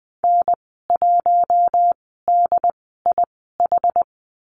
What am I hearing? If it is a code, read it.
Morse code, heard as a keyed tone, about 20 wpm, spaced as for 15 wpm, 700 Hz. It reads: N1DIH
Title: 076 Lesson 13g - Letter L - Call Signs 20wpm